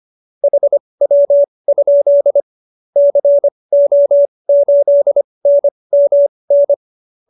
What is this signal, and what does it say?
Morse code, heard as a keyed tone, about 25 wpm, spaced as for 20 wpm, 570 Hz. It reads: HW? CO8NMN